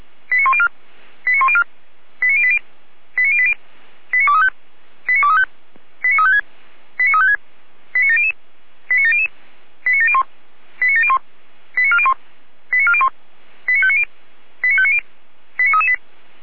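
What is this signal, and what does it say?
Five-tone selective call (ZVEI), heard as quick runs of five tones, tones of about 70 ms. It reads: 89104 89104 89080 89080 89235 89235 89356 89356 89700 89700 89701 89701 89401 89401 89590 89590 89338
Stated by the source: ZVEI five-tone selective call 89092 received by SDR.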